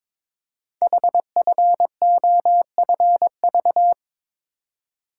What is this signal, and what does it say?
Morse code, heard as a keyed tone, about 22 wpm, 700 Hz. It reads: HFOFV
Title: Code Groups - 5 Letters - 22wpm